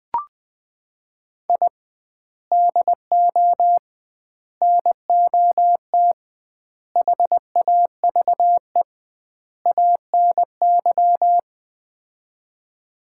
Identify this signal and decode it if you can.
Morse code, heard as a keyed tone, about 20 wpm, 700 Hz. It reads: I DO NOT HAVE ANY
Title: 2x Word-Spacing - Sets of 5 Words - 20wpm